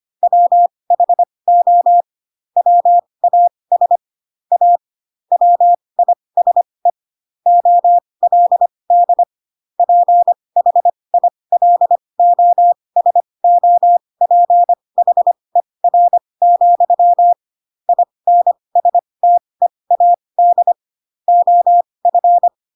Morse code, heard as a keyed tone, about 25 wpm, spaced as for 20 wpm, 700 Hz. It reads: WHO WAS A WISE OLD PHILOSOPHER, INSTEAD OF
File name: Pinocchio / Pinocchio ch4 15 25 ik1hge